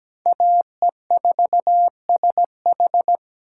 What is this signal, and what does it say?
Morse code, heard as a keyed tone, about 17 wpm, 700 Hz. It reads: AE4SH
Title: USA Call Signs - 5 Characters - 8x Character Spacing - 17wpm